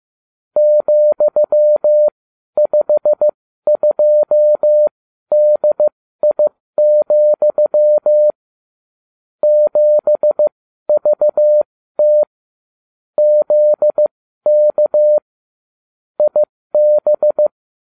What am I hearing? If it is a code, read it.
Morse code, heard as a keyed tone, about 15 wpm, spaced as for 12 wpm, 600 Hz. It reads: ,52DI, 7VT ZK IB